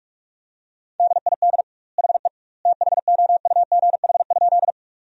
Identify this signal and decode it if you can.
Morse code, heard as a keyed tone, about 42 wpm, 700 Hz. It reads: DID HE THOUGH?
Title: 2x Word-Spacing - Sentences from Top 400 Words - 45wpm